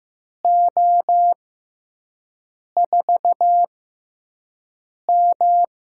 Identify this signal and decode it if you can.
Morse code, heard as a keyed tone, about 15 wpm, 700 Hz. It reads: O 4 M
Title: USA Call Signs - 6x Character Spacing - 15wpm